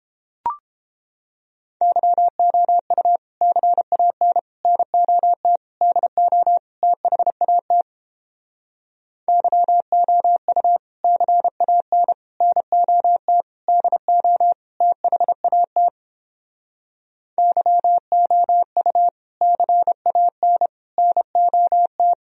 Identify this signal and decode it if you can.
Morse code, heard as a keyed tone, about 29 wpm, 700 Hz. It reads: YOU CAN NOT DO THAT YOU CAN NOT DO THAT YOU CAN NOT